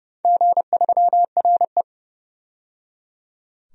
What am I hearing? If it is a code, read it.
Morse code, heard as a keyed tone, about 30 wpm, 700 Hz. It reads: G3RE
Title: DXCC-100 Call Signs - 4 Characters - 30wpm